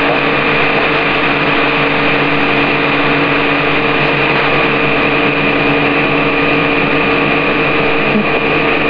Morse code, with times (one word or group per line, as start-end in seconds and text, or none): none